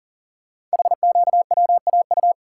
Morse code, heard as a keyed tone, about 42 wpm, 700 Hz. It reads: HQWAU
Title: Code Groups - 5 Letters - 40wpm